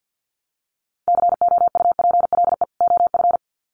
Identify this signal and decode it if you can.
Morse code, heard as a keyed tone, about 50 wpm, 700 Hz. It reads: COUPLE OF
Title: Sets of 2 Words 50wpm